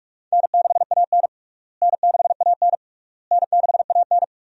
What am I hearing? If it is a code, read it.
Morse code, heard as a keyed tone, about 46 wpm, 700 Hz. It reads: N6AN N6AN N6AN
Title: USA Call Signs - 4 Characters - Repeated 3x - 45wpm